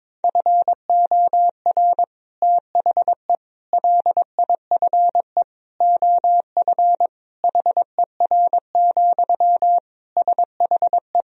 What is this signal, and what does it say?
Morse code, heard as a keyed tone, about 22 wpm, 700 Hz. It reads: FOR THE LIFE OF HER, SHE
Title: Sentences from Top 1000 Words - Review Part 2 - 22wpm